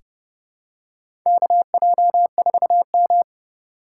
Morse code, heard as a keyed tone, about 32 wpm, 700 Hz. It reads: KJ4M
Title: USA Call Signs - 4 Characters - 2x Character Spacing - 30wpm